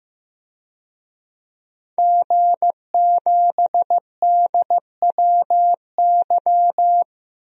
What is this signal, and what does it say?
Morse code, heard as a keyed tone, about 15 wpm, 700 Hz. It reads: G7DWY
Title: DXCC-100 Call Signs - 5 Characters - 15wpm